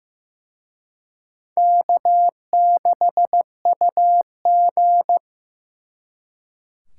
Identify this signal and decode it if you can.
Morse code, heard as a keyed tone, about 18 wpm, 700 Hz. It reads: K6UG
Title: Speed Racing - USA Call Signs 15wpm